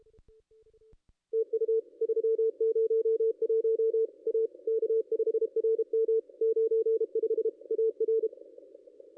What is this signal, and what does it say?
Morse code, heard as a keyed tone, about 29 wpm, 440 Hz. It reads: TU 301 A K5RM 95 AR
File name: contests / ss-cw-k5tr-op-n5rz -0400